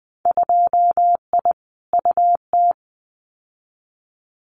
Morse code, heard as a keyed tone, about 20 wpm, 700 Hz. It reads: 2I UT
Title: Winter Field Day Exchanges 20wpm